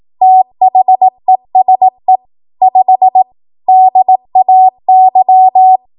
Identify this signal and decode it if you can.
Morse code, heard as a keyed tone, about 18 wpm, 750 Hz. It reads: THESE 5 DAY
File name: Archive / 18 WPM / 18WPM